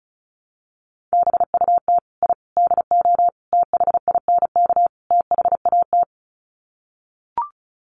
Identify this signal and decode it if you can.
Morse code, heard as a keyed tone, about 34 wpm, 700 Hz. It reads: BUT I DO THINK THAT
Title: Sets of 5 Words Encore2 35wpm